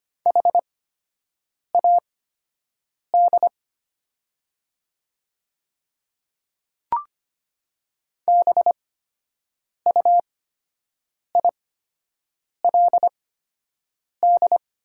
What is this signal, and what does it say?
Morse code, heard as a keyed tone, about 25 wpm, spaced as for 7 wpm, 690 Hz. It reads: HAD BUILD
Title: 8x Character Spacing - Top 300 Words - 25wpm